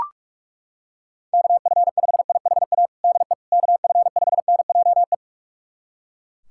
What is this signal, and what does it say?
Morse code, heard as a keyed tone, about 42 wpm, 700 Hz. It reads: KF5IHA DE KF5NJE
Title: USA Callsign DE Callsign 45wpm